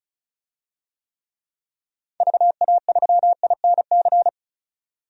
Morse code, heard as a keyed tone, about 34 wpm, 700 Hz. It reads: VA3INC
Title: DXCC-100 Call Signs - 6 Characters - 35wpm